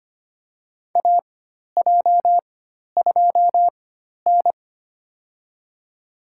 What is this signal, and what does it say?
Morse code, heard as a keyed tone, about 25 wpm, spaced as for 12 wpm, 700 Hz. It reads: AJ2N
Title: USA Call Signs - 4 Characters - 4x Character Spacing - 25wpm